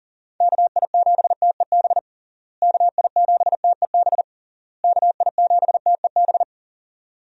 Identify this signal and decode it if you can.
Morse code, heard as a keyed tone, about 40 wpm, 700 Hz. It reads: KI7TEB KI7TEB KI7TEB
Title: USA Call Signs - 6 Characters - Repeated 3x - 40wpm